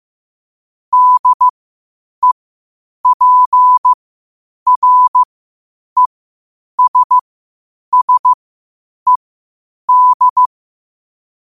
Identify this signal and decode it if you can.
Morse code, heard as a keyed tone, about 15 wpm, spaced as for 8 wpm, 1000 Hz. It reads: DEPRESSED
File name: qotdfiles / 8wpm